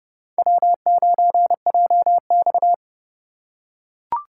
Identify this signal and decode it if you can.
Morse code, heard as a keyed tone, about 29 wpm, 700 Hz. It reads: W9JX